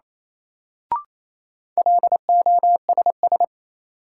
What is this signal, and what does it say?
Morse code, heard as a keyed tone, about 28 wpm, 700 Hz. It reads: LOSS
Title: Sentences from Top 1400 Words - New Word Drill - 28wpm